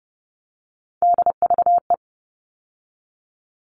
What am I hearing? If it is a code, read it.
Morse code, heard as a keyed tone, about 30 wpm, 700 Hz. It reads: DVE
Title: All Cut Numbers - 3-Digit 30wpm